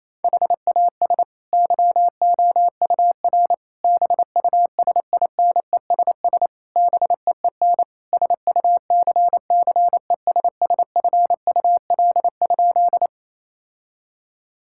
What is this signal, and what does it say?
Morse code, heard as a keyed tone, about 27 wpm, 700 Hz. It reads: HAS YOUR BUSINESS BEEN SUCCESSFUL?
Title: Sentences from Top 1000 Words - Review Part 6 - 28wpm